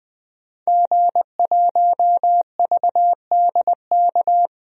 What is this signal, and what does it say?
Morse code, heard as a keyed tone, about 20 wpm, 700 Hz. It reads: G1VDK
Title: DXCC-100 Call Signs - 5 Characters - Repeated 3x - 20wpm